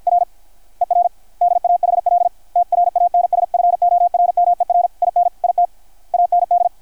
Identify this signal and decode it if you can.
Morse code, heard as a keyed tone, about 53 wpm, 690 Hz, with radio noise. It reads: R ER DAHL TRANSFORMER IN IT AND
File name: archived / CRSnet